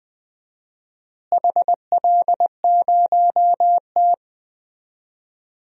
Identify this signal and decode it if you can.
Morse code, heard as a keyed tone, about 20 wpm, 700 Hz. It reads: HL0T
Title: DXCC-100 Call Signs - 4 Characters - Repeated 3x - 20wpm